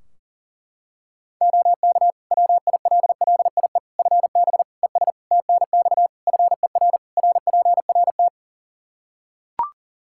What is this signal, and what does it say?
Morse code, heard as a keyed tone, about 37 wpm, 700 Hz. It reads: OK WILLIE FB ES TNX FER RPRT
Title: Practice QSO - Exchange 2 - Info - 40wpm